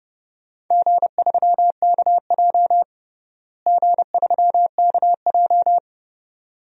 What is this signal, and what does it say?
Morse code, heard as a keyed tone, about 31 wpm, 700 Hz. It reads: G3KJ G3KJ
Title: DXCC-100 Call Signs - 4 Characters - Repeated 2x - 30wpm